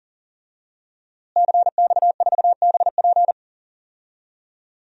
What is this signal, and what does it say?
Morse code, heard as a keyed tone, about 40 wpm, 700 Hz. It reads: CX4BP
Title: DXCC-100 Call Signs - 5 Characters - Repeated 3x - 40wpm